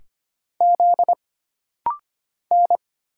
Morse code, heard as a keyed tone, about 25 wpm, 700 Hz. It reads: Z N
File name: Single Character (Letter:Number:Punctuation:Prosign) - Rapid-Fire - 25wpm